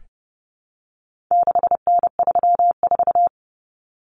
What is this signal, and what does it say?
Morse code, heard as a keyed tone, about 30 wpm, 700 Hz. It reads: 6N34
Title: Common Cut Numbers - 4-Digit 30wpm